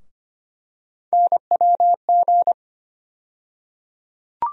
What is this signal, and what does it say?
Morse code, heard as a keyed tone, about 25 wpm, 700 Hz. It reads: NWG
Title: USA Call Sign Postfixes - 3 Letters 25wpm